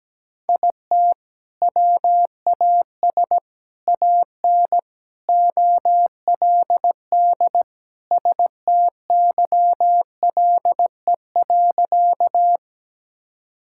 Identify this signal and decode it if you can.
Morse code, heard as a keyed tone, about 17 wpm, 700 Hz. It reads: IT WAS AN OLD STYLE.
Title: Sentences from Top 1000 Words - Review Part 5 - 17wpm